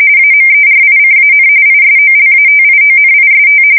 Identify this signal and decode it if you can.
RTTY, about 45.45 baud. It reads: AMERICAN ISOLATIONIST
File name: fsk170 / usnavy4